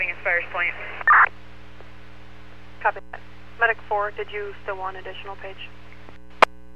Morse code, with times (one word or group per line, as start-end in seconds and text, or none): none